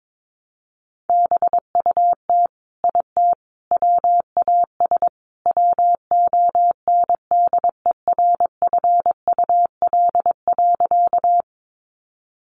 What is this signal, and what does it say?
Morse code, heard as a keyed tone, about 22 wpm, 700 Hz. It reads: BUT IT WAS WONDERFUL.
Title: Sentences from Top 1100 Words 22wpm